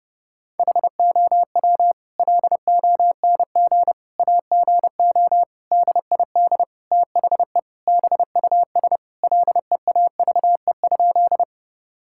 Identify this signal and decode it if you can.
Morse code, heard as a keyed tone, about 30 wpm, 700 Hz. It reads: HOW LONG AGO DID THE BUS LEAVE?
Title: Sentences from Top 1000 Words - Review Part 6 - 30wpm